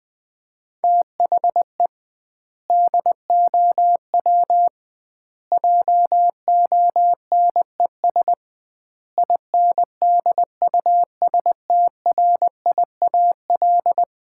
Morse code, heard as a keyed tone, about 21 wpm, 700 Hz. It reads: THE DOW JONES INDUSTRIAL
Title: 2x Word-Spacing - Sets of 5 Words - 20wpm